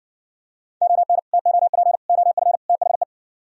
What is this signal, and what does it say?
Morse code, heard as a keyed tone, about 62 wpm, 700 Hz. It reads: ON TOP OF THE